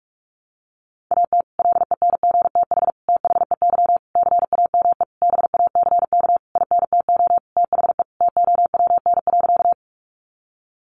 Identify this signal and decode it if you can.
Morse code, heard as a keyed tone, about 44 wpm, 700 Hz. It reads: AT LENGTH THEY CAME BACK INTO THE TOWN.
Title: Sentences from Top 1000 Words 45wpm